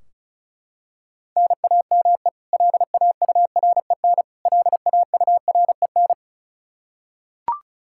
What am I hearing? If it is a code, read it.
Morse code, heard as a keyed tone, about 34 wpm, 700 Hz. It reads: NAME LAUREN LAUREN
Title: Practice QSO - Exchange 1 - Beginning - 35wpm